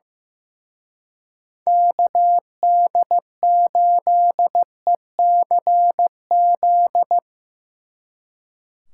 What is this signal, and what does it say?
Morse code, heard as a keyed tone, about 17 wpm, 700 Hz. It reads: KD8ECZ
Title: Speed Racing - USA 2x3 Call Signs 15wpm